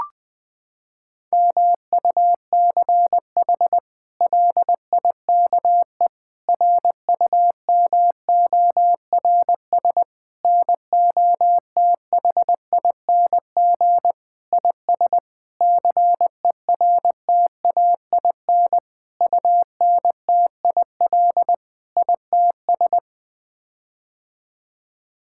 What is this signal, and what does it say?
Morse code, heard as a keyed tone, about 20 wpm, 700 Hz. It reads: MUCH LIKE RUMORS NOTHING IS CERTAIN UNTIL ITS
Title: Sets of 8 Words - 20wpm